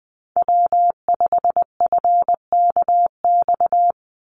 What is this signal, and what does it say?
Morse code, heard as a keyed tone, about 20 wpm, 700 Hz. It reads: W5FKX
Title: Call Signs - 20wpm